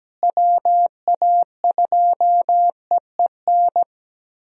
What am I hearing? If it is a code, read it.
Morse code, heard as a keyed tone, about 17 wpm, 700 Hz. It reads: WA2EEN
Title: USA Call Signs - 6 Characters - Repeated 3x - 17wpm